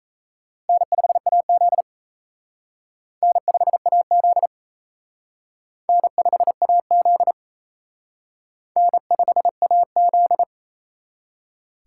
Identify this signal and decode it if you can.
Morse code, heard as a keyed tone, about 33 wpm, 700 Hz. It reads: N5AZ N5AZ N5AZ N5AZ
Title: Speed Racing - USA 1x2 Call Signs 28wpm